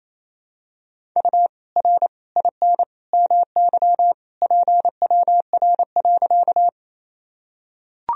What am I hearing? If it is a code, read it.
Morse code, heard as a keyed tone, about 28 wpm, 700 Hz. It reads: U R IN MY PWR.
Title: Sentences from Top 500 Words and Abbreviations 28wpm